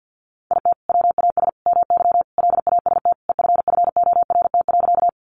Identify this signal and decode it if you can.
Morse code, heard as a keyed tone, about 50 wpm, 700 Hz. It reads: IT WAS MY LAST EFFORT.
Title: Sentences from Top 1200 Words 50wpm